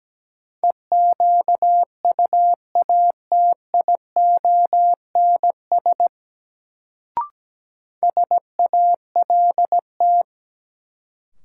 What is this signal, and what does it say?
Morse code, heard as a keyed tone, about 17 wpm, 700 Hz. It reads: EQUATIONS SALT
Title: Sets of 5 Words - New Word Drill - 17wpm